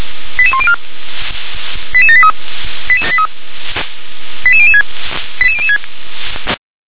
Five-tone selective call (ZVEI), heard as quick runs of five tones, tones of about 70 ms. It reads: 88104 88783 88006 88006
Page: /357/SDR/ZVEI/88101/2022.10.25/